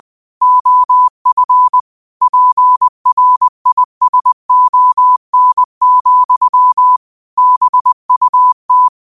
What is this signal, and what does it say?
Morse code, heard as a keyed tone, about 20 wpm, 1000 Hz. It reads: OF PRISON, BUT
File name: qotdfiles / 20wpm